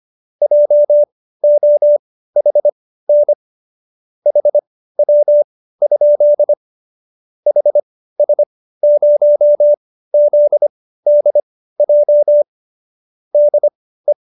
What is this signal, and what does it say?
Morse code, heard as a keyed tone, about 25 wpm, spaced as for 15 wpm, 590 Hz. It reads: JOHN HW? HS0ZDJ DE